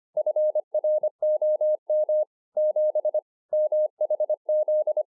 Morse code, heard as a keyed tone, about 24 wpm, 610 Hz. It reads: FROM 7 MHZ